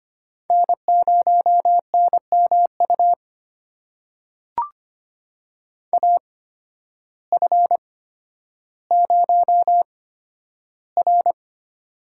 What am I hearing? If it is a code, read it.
Morse code, heard as a keyed tone, about 25 wpm, spaced as for 12 wpm, 700 Hz. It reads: N0NMU AF0R
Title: USA Call Signs - 8x Character Spacing - 25wpm